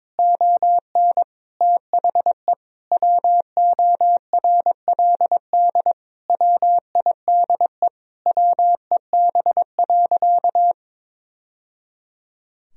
Morse code, heard as a keyed tone, about 22 wpm, 700 Hz. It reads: ON THE WORLD WIDE WEB.
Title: Sentences from Top 1000 Words - Review Part 4 - 22wpm